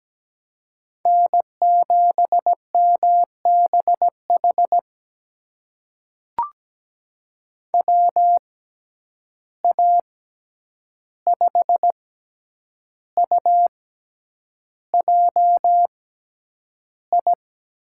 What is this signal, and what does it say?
Morse code, heard as a keyed tone, about 17 wpm, spaced as for 8 wpm, 700 Hz. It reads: N7MBH WA5UJI
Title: USA Call Signs - 6x Character Spacing - 17wpm